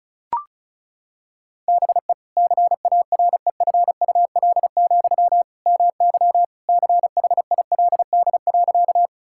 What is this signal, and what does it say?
Morse code, heard as a keyed tone, about 34 wpm, 700 Hz. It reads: BE CAREFUL, MY CHILD.
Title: Sentences from Top 1600 Words 35wpm